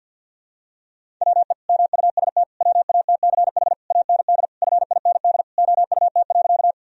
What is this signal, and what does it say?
Morse code, heard as a keyed tone, about 46 wpm, 700 Hz. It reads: WE MUST WATCH AND FIND OUT.
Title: Sentences from Top 1000 Words - Review Part 1 - 50wpm